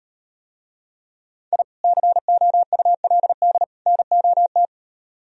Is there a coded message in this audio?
Yes: Morse code from 1.5 to 4.7 s, heard as a keyed tone, about 38 wpm, 700 Hz. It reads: I COULD NOT